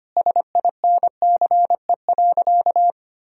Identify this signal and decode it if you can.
Morse code, heard as a keyed tone, about 25 wpm, 700 Hz. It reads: SINCE.
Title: Sentences from Top 1000 Words - Review Part 1 - 25wpm